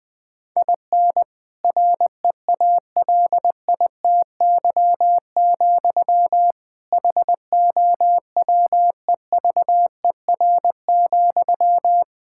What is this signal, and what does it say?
Morse code, heard as a keyed tone, about 20 wpm, 700 Hz. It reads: IN REALITY, HOWEVER,
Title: Sentences from Top 1600 Words 20wpm